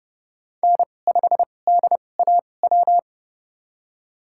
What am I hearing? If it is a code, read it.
Morse code, heard as a keyed tone, about 30 wpm, spaced as for 22 wpm, 700 Hz. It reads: N5DAW
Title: USA Call Signs - 5 Characters - 2x Character Spacing - 30wpm